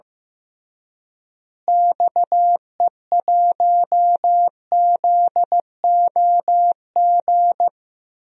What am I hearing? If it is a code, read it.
Morse code, heard as a keyed tone, about 15 wpm, 700 Hz. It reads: XE1ZOG
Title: DXCC-100 Call Signs - 6 Characters - Repeated 3x - 15wpm